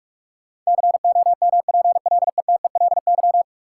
Morse code, heard as a keyed tone, about 46 wpm, 700 Hz. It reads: COMPLETELY
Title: Sentences from Top 1100 Words - New Word Drill - 45wpm